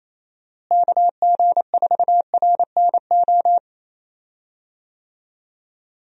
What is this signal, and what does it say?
Morse code, heard as a keyed tone, about 27 wpm, 700 Hz. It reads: KG4RNO
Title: USA Call Signs - 6 Characters - Repeated 3x - 28wpm